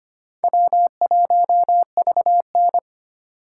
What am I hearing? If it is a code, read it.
Morse code, heard as a keyed tone, about 25 wpm, 700 Hz. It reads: W1VN
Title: USA Call Signs - 4 Characters - 8x Character Spacing - 25wpm